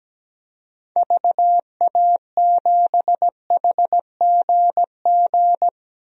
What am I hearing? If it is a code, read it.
Morse code, heard as a keyed tone, about 17 wpm, 700 Hz. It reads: VA7HGG
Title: DXCC-100 Call Signs - 6 Characters - 17wpm